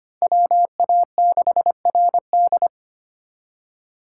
Morse code, heard as a keyed tone, about 32 wpm, 700 Hz. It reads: WA6RD